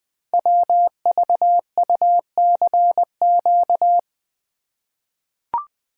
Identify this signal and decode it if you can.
Morse code, heard as a keyed tone, about 21 wpm, 700 Hz. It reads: WVUCQ
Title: Code Groups - 5 Letters - 20wpm